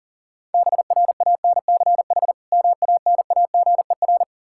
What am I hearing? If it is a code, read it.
Morse code, heard as a keyed tone, about 40 wpm, 700 Hz. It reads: BRANCH MANAGER
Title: Practice QSO - Exchange 3 - More Info - 40wpm